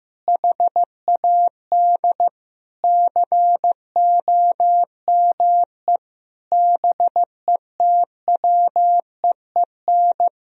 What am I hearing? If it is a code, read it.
Morse code, heard as a keyed tone, about 15 wpm, 700 Hz. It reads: HAD COME BETWEEN
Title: Sentences from Top 1000 Words - Review Part 1 - 15wpm